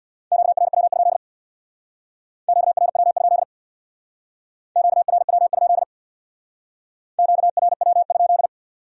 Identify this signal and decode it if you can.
Morse code, heard as a keyed tone, about 62 wpm, 700 Hz. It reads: QRW? QRW? QRW? QRW?
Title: Speed-Racing Q-Codes - 50wpm